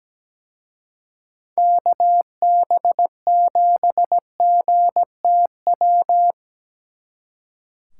Morse code, heard as a keyed tone, about 20 wpm, 700 Hz. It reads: KB7GTW
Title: Speed Racing - USA 2x3 Call Signs 17wpm